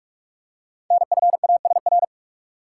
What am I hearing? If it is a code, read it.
Morse code, heard as a keyed tone, about 44 wpm, 700 Hz. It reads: NFASR
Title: Code Groups - 5 Letters - 45wpm